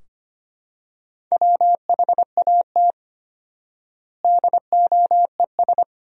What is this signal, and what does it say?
Morse code, heard as a keyed tone, about 25 wpm, 700 Hz. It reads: WHAT DOES
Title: 4x Word-Spacing - Sentences from Top 200 Words - 25wpm